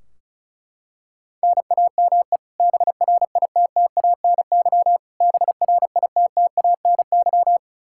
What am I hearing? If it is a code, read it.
Morse code, heard as a keyed tone, about 36 wpm, 700 Hz. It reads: NAME BRITTANY BRITTANY
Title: Practice QSO - Exchange 1 - Beginning - 35wpm